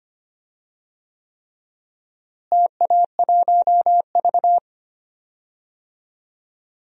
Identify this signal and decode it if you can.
Morse code, heard as a keyed tone, about 25 wpm, 700 Hz. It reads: TA1V
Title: DXCC-100 Call Signs - 4 Characters - Repeated 3x - 25wpm